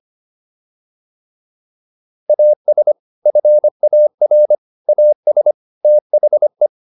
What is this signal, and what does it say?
Morse code, heard as a keyed tone, about 25 wpm, 600 Hz. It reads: AS FAR AS THE